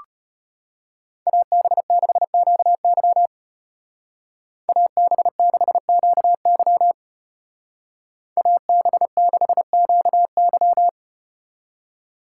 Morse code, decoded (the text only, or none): AB6QY AB6QY AB6QY